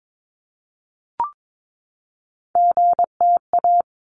Morse code, heard as a keyed tone, about 22 wpm, 700 Hz. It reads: GTA
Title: ARRL Sections 22wpm